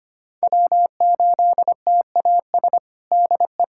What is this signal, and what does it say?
Morse code, heard as a keyed tone, about 25 wpm, 700 Hz. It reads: W8TAS DE
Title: USA Callsign DE Callsign 25wpm